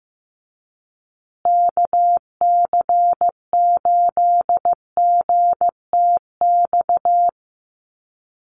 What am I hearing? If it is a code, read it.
Morse code, heard as a keyed tone, about 15 wpm, 700 Hz. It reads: KC8GTX
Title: USA Call Signs Repeated 2x 15wpm